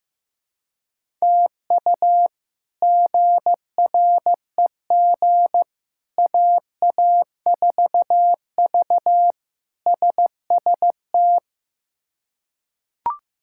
Morse code, heard as a keyed tone, about 15 wpm, 700 Hz. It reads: TU GREG AA4V SST
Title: K1USN SST QSO Practice - 15wpm